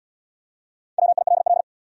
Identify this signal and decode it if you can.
Morse code, heard as a keyed tone, about 66 wpm, 700 Hz. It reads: WILL